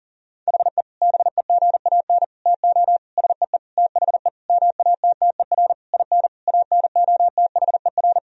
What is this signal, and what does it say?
Morse code, heard as a keyed tone, about 40 wpm, 700 Hz. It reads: HE BEGAN TO SEE THE MATTER IN ANOTHER
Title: Sentences from Top 1000 Words - Review Part 1 - 40wpm